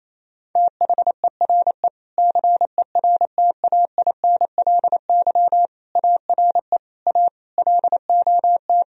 Morse code, heard as a keyed tone, about 28 wpm, 700 Hz. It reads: THERE CERTAINLY ARE A LOT
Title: Sentences from Top 1000 Words - Review Part 3 - 28wpm